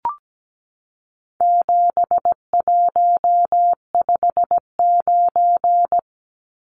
Morse code, heard as a keyed tone, about 17 wpm, 700 Hz. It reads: 7159
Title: Numbers - 4-Digit 17wpm